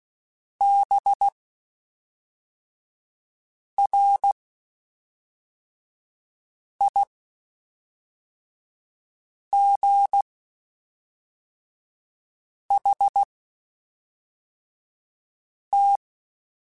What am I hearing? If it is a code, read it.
Morse code, heard as a keyed tone, about 16 wpm, spaced as for 3 wpm, 800 Hz. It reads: BRIGHT
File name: Spaced Characters / Spaced Characters-mp3s / Weather5 5 sp @5